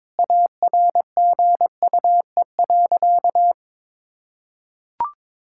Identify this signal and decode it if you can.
Morse code, heard as a keyed tone, about 21 wpm, 700 Hz. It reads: ARGUE.
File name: Sentences with 7-Word Phrases - 22wpm